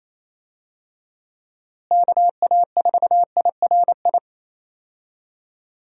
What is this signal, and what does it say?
Morse code, heard as a keyed tone, about 27 wpm, 700 Hz. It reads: KA4IRI
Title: USA Call Signs - 6 Characters - Repeated 2x - 28wpm